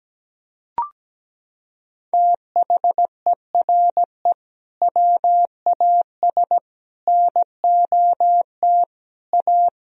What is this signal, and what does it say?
Morse code, heard as a keyed tone, about 17 wpm, 700 Hz. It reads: THERE WAS NOT A
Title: Sentences from Top 1000 Words - Review Part 4 - 17wpm